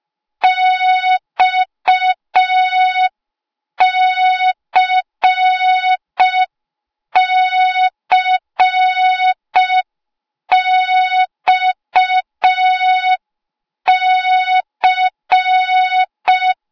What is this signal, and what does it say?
Morse code, heard as a keyed tone, about 5 wpm, 3700 Hz. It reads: XCCXC